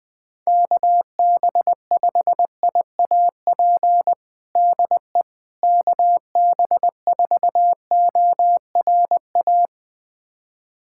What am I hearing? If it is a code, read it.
Morse code, heard as a keyed tone, about 21 wpm, 700 Hz. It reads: KB5IAP DE KB4ORA
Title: USA Callsign DE Callsign - 6 Characters - 20wpm